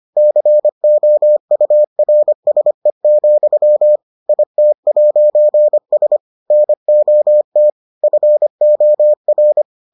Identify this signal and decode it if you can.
Morse code, heard as a keyed tone, about 25 wpm, 600 Hz. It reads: COURSE, IT'S NOT FOR